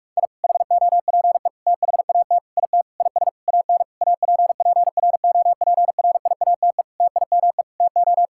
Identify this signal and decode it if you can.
Morse code, heard as a keyed tone, about 46 wpm, 700 Hz. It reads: I HOPE THAT IT IS AN APPROPRIATE TIME TO
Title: Sentences from Top 1000 Words - Review Part 5 - 45wpm